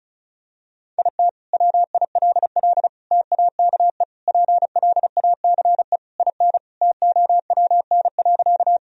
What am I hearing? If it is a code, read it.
Morse code, heard as a keyed tone, about 34 wpm, 700 Hz. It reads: IT WILL TAKE PLACE IN TOWN.
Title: Sentences from Top 1000 Words - Review Part 2 - 35wpm